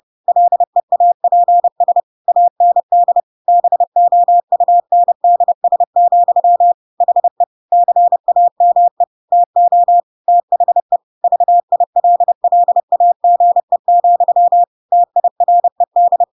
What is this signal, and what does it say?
Morse code, heard as a keyed tone, about 30 wpm, 700 Hz. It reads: LEAPS AND BOUNDS, HE CAME TO THE VILLAGE, TIRED
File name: Pinocchio / Pinocchio ch6 30 30 ik1hge